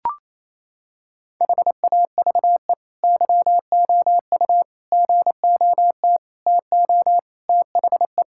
Morse code, heard as a keyed tone, about 28 wpm, 700 Hz. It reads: HAVE YOU GOT TO THE
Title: Sentences from Top 1000 Words - Review Part 6 - 28wpm